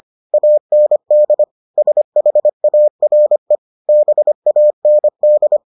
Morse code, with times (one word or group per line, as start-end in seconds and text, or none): 0.3-1.4 AND
1.8-3.5 SHARE
3.9-5.6 BAND